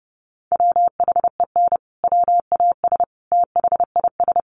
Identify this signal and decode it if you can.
Morse code, heard as a keyed tone, about 30 wpm, 700 Hz. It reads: WHEN WAS THIS
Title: Sentences from Top 1200 Words 30wpm